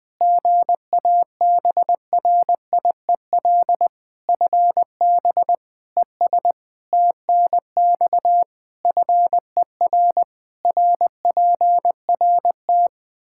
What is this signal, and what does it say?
Morse code, heard as a keyed tone, about 21 wpm, 700 Hz. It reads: GABRIEL FB ES TNX FER RPRT
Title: Practice QSO - Exchange 2 - Info - 20wpm